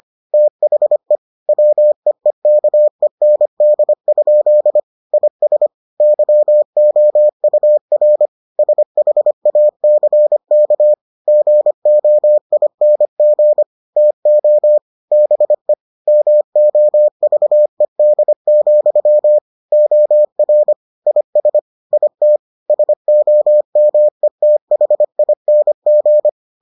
Morse code, heard as a keyed tone, about 25 wpm, 600 Hz. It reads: THE WEEKEND? IS YOUR SHACK GOING TO BE MOVED, OR IS IT SOMETHING